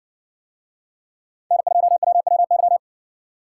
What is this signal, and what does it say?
Morse code, heard as a keyed tone, about 60 wpm, 700 Hz. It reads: N2WWY